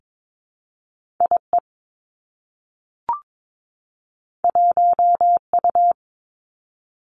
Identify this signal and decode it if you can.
Morse code, heard as a keyed tone, about 22 wpm, 700 Hz. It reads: I E 1 U
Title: Two Challenging Letter-Number Combinations 22wpm